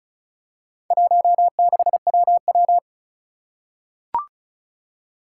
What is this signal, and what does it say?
Morse code, heard as a keyed tone, about 36 wpm, 690 Hz. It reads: 16WW
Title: Four Challenging Letter-Number Combinations 35wpm